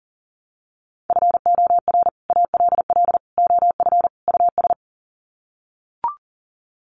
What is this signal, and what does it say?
Morse code, heard as a keyed tone, about 40 wpm, 700 Hz. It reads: FOR ALL OF US
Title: Sets of 4 Words 40wpm